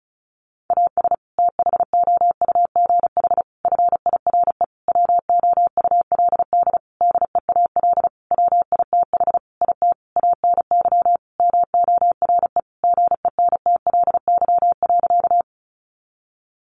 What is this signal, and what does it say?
Morse code, heard as a keyed tone, about 34 wpm, 700 Hz. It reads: AS THOUGH FIRE WOULD DEAL WITH IT ANY MORE GENTLY.
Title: Sentences from Top 1500 Words 35wpm